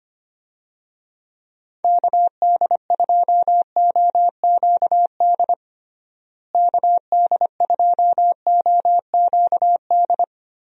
Morse code, heard as a keyed tone, about 25 wpm, 700 Hz. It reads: KD2OQD KD2OQD
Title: USA Call Signs - 6 Characters - Repeated 3x - 25wpm